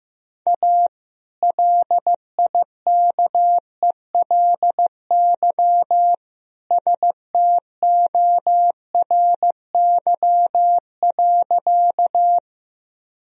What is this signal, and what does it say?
Morse code, heard as a keyed tone, about 15 wpm, 700 Hz. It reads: A LIKELY STORY.
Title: Sentences from Top 1000 Words - Review Part 4 - 15wpm